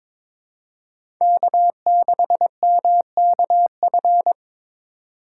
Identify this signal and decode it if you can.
Morse code, heard as a keyed tone, about 22 wpm, 700 Hz. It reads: K6MKF
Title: K1USN SST QSO Practice - 22wpm